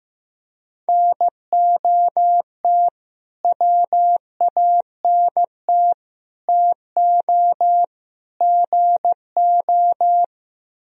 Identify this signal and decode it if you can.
Morse code, heard as a keyed tone, about 16 wpm, 700 Hz. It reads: NOT WANT TO GO
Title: Speed-Racing - Sets of 4 Words 15wpm